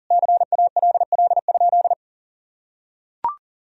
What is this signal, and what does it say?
Morse code, heard as a keyed tone, about 37 wpm, 700 Hz. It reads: CALL?
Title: Sentences from Top 100 Words and Abbreviations 40wpm